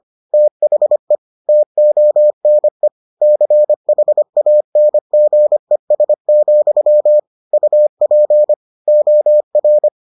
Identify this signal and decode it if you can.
Morse code, heard as a keyed tone, about 25 wpm, 600 Hz. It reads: THE TONE CHANGES, UP OR